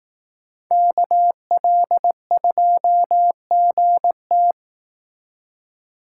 Morse code, heard as a keyed tone, about 18 wpm, 700 Hz. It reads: KL2GT
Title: Speed Racing - USA Call Signs 15wpm